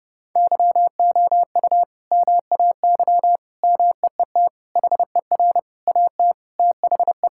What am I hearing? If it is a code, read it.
Morse code, heard as a keyed tone, about 30 wpm, 700 Hz. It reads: YOU MAY MEET HER AT THE
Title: Sentences from Top 1600 Words 30wpm